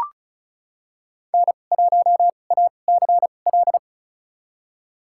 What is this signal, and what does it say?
Morse code, heard as a keyed tone, about 36 wpm, spaced as for 25 wpm, 700 Hz. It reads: N1ACL
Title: USA Call Signs - 5 Characters - 2x Character Spacing - 35wpm